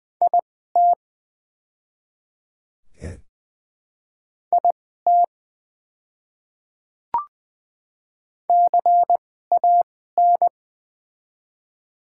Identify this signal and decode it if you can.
Morse code, heard as a keyed tone, about 20 wpm, spaced as for 14 wpm, 700 Hz. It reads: IT IT CAN
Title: 2x Character Spacing - Top 100 Words - 20wpm